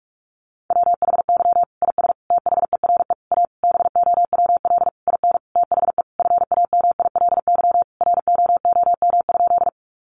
Morse code, heard as a keyed tone, about 44 wpm, 700 Hz. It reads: WHY IS THERE A BOWL IN THE FAMILY ROOM?
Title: Sentences from Top 1200 Words 45wpm